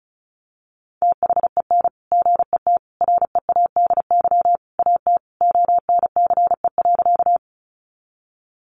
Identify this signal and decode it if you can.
Morse code, heard as a keyed tone, about 34 wpm, 700 Hz. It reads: THEN GET READY AT ONCE.
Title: Sentences from Top 400 Words 35wpm